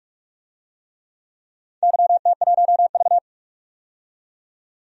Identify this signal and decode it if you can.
Morse code, heard as a keyed tone, about 46 wpm, 700 Hz. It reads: YT1V
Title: DXCC-100 Call Signs - 4 Characters - 45wpm